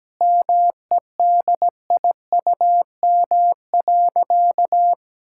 Morse code, heard as a keyed tone, about 17 wpm, 700 Hz. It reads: MEDIUM.
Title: Sentences from Top 1000 Words - Review Part 6 - 17wpm